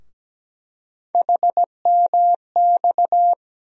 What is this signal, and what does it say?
Morse code, heard as a keyed tone, about 17 wpm, 700 Hz. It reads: HMX
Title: USA Call Sign Postfixes - 3 Letters 17wpm